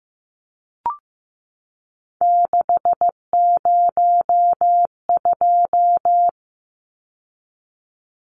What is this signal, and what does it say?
Morse code, heard as a keyed tone, about 15 wpm, 700 Hz. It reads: 602
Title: Numbers - 3-Digit 15wpm